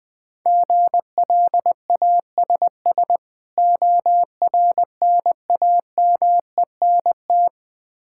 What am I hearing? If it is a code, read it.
Morse code, heard as a keyed tone, about 20 wpm, 700 Hz. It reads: GLASS ORNAMENT
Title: Difficult Words with Hints - 20wpm